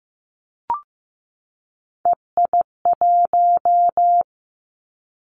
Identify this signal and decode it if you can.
Morse code, heard as a keyed tone, about 15 wpm, 700 Hz. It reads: EI1
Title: Callsign Prefixes for Top 50 DXCC - 15wpm